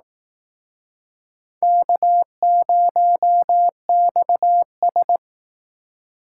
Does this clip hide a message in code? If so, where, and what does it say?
Yes: Morse code from 1.6 to 5.2 s, heard as a keyed tone, about 19 wpm, 700 Hz. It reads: K0XS